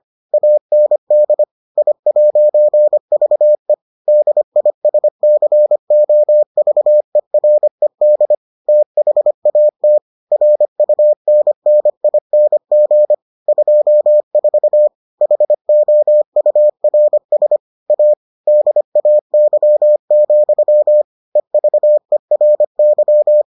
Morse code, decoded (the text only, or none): AND I'VE DISCOVERED THAT RUNNING 24 HOURS A DAY, EVERY